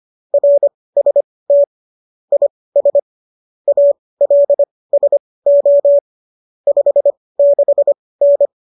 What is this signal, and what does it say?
Morse code, heard as a keyed tone, about 25 wpm, spaced as for 18 wpm, 570 Hz. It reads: RST IS ALSO 56N
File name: qso201 18